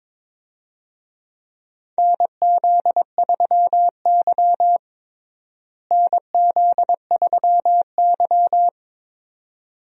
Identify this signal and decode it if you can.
Morse code, heard as a keyed tone, about 21 wpm, 700 Hz. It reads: NZ3Y NZ3Y
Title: USA Call Signs - 4 Characters - Repeated 2x - 22wpm